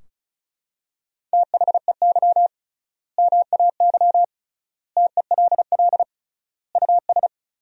Morse code, heard as a keyed tone, about 34 wpm, 700 Hz. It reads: THEY MAY TELL US
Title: 3x Word-Spacing - Sentences from Top 300 Words - 35wpm